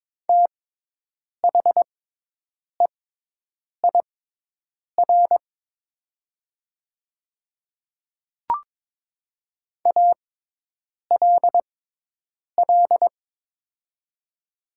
Morse code, heard as a keyed tone, about 22 wpm, spaced as for 8 wpm, 700 Hz. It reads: THEIR ALL
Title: 6x Character Spacing - Top 100 Words - 22wpm